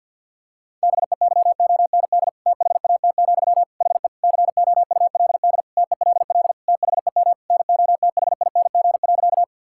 Morse code, heard as a keyed tone, about 52 wpm, 700 Hz. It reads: BEYOND THAT, HE COULD TELL THEM NOTHING.
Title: Sentences from Top 1000 Words - Review Part 6 - 50wpm